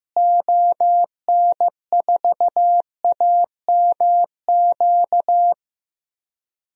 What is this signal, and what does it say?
Morse code, heard as a keyed tone, about 15 wpm, 700 Hz. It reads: ON4AMQ
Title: DXCC-100 Call Signs - 6 Characters - Repeated 2x - 15wpm